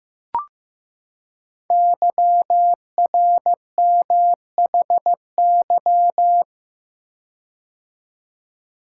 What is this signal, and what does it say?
Morse code, heard as a keyed tone, about 15 wpm, 700 Hz. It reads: YRMHY
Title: Code Groups - 5 Letters - 15wpm